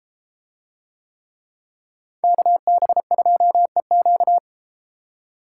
Morse code, heard as a keyed tone, about 31 wpm, 700 Hz. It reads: KB2EQ